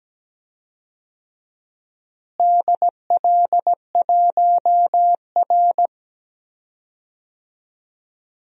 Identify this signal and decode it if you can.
Morse code, heard as a keyed tone, about 17 wpm, 700 Hz. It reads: DL1R
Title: DXCC-100 Call Signs - 4 Characters - Repeated 3x - 17wpm